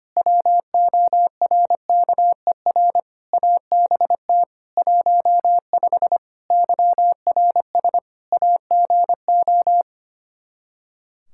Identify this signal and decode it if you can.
Morse code, heard as a keyed tone, about 25 wpm, 700 Hz. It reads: WORKER ABT 15 YRS AGO
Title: K1USN SST QSO Practice 25wpm